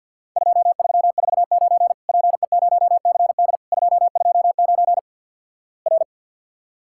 Morse code, heard as a keyed tone, about 50 wpm, 700 Hz. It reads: 2349 PE0CD 229 R
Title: Worked All Europe DX Contest - QTC Exchange with Repeats - 50wpm